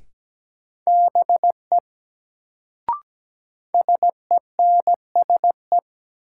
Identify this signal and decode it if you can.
Morse code, heard as a keyed tone, about 17 wpm, 700 Hz. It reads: BE SENSE
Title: Sets of 2 Words - New Word Drill - 17wpm